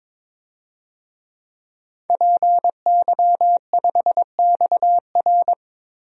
Morse code, heard as a keyed tone, about 22 wpm, 700 Hz. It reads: PY5XR
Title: DXCC-100 Call Signs - 5 Characters - 22wpm